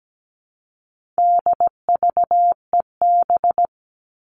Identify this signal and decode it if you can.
Morse code, heard as a keyed tone, about 17 wpm, 700 Hz. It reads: DVEB
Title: All Cut Numbers - 4-Digit 17wpm